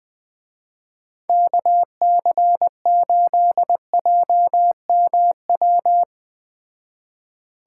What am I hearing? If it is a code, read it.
Morse code, heard as a keyed tone, about 21 wpm, 700 Hz. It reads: KC8JMW
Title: Speed Racing - USA Call Signs 17wpm